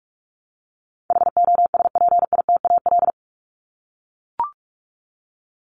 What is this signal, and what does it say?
Morse code, heard as a keyed tone, about 44 wpm, 700 Hz. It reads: HOSPITAL
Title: Top 1751-2000 Words 45wpm